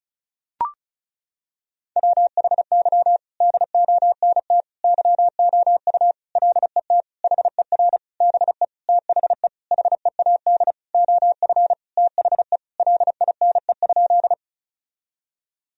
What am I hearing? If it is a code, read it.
Morse code, heard as a keyed tone, about 36 wpm, 700 Hz. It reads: WHY DONT YOU LET HER BE THE HEAD OF THE LINE?
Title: Sentences from Top 1000 Words - Review Part 1 - 35wpm